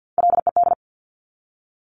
Morse code, heard as a keyed tone, about 50 wpm, 700 Hz. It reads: LED
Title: Top Three Letter Combinations in Words 50wpm